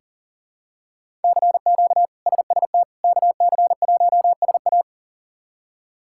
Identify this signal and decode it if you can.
Morse code, heard as a keyed tone, about 37 wpm, 700 Hz. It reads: CQ SST KC1SA
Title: K1USN SST QSO Practice - 40wpm